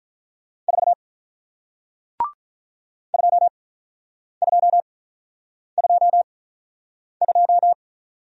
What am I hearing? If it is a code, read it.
Morse code, heard as a keyed tone, about 48 wpm, spaced as for 9 wpm, 700 Hz. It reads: 4 2222